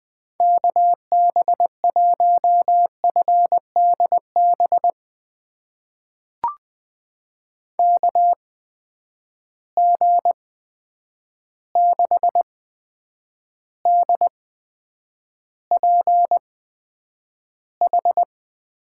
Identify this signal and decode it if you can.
Morse code, heard as a keyed tone, about 20 wpm, spaced as for 9 wpm, 700 Hz. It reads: KB1FDB KG6DPH
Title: USA Call Signs - 6 Characters - 8x Character Spacing - 20wpm